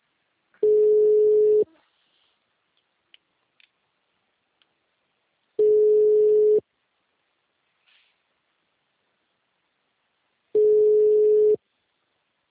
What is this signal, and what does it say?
Ringback tone (Europe, 425 Hz).
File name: out- -2013- - - .11